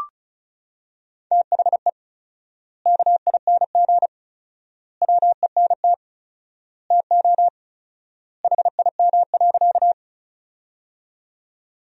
Morse code, heard as a keyed tone, about 36 wpm, 700 Hz. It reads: THE KING WENT TO HIM.
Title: 4x Word-Spacing - Sentences from Top 400 Words - 35wpm